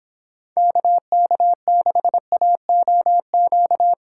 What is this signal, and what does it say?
Morse code, heard as a keyed tone, about 29 wpm, 700 Hz. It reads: KK6AOQ